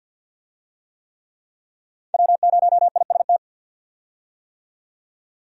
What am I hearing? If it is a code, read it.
Morse code, heard as a keyed tone, about 50 wpm, 700 Hz. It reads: W0IST
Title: USA Call Signs - 5 Characters - 50wpm